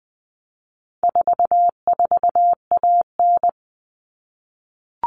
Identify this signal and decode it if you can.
Morse code, heard as a keyed tone, about 20 wpm, 700 Hz. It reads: 44AN
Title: All Cut Numbers - 4-Digit 20wpm